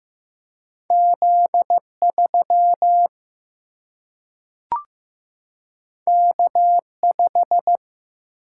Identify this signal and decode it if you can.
Morse code, heard as a keyed tone, about 15 wpm, 700 Hz. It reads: Z3 K5
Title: Single-Letter Single-Number 15wpm